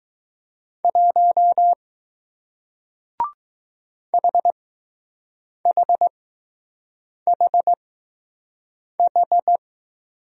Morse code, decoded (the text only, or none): 1 HHHH